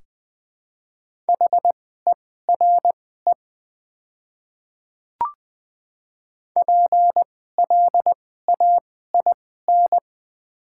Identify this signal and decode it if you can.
Morse code, heard as a keyed tone, about 20 wpm, spaced as for 14 wpm, 700 Hz. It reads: HERE PLAIN
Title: 2x Character Spacing - Top 500 Words - 20wpm